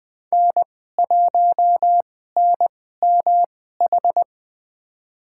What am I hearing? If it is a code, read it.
Morse code, heard as a keyed tone, about 20 wpm, spaced as for 15 wpm, 700 Hz. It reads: N1NMH
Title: USA Call Signs - 5 Characters - 2x Character Spacing - 20wpm